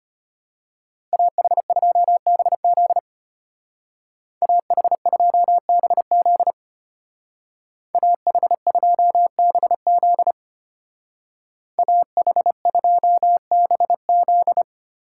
Morse code, heard as a keyed tone, about 33 wpm, 700 Hz. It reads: AH2BZ AH2BZ AH2BZ AH2BZ